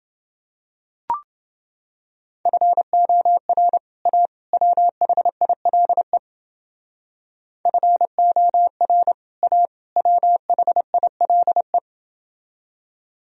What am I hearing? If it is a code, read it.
Morse code, heard as a keyed tone, about 29 wpm, 700 Hz. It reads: FOR A WHILE FOR A WHILE